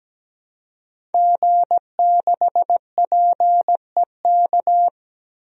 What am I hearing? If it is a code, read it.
Morse code, heard as a keyed tone, about 17 wpm, 700 Hz. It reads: G6PEK
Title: DXCC-100 Call Signs - 5 Characters - 17wpm